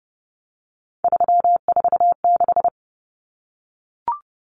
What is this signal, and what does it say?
Morse code, heard as a keyed tone, about 30 wpm, 700 Hz. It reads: 346
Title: Common Cut Numbers - 3-Digit 30wpm